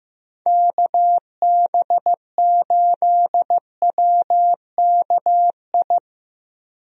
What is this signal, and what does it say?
Morse code, heard as a keyed tone, about 15 wpm, 700 Hz. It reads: KB8WKI
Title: USA Call Signs - 6 Characters - 15wpm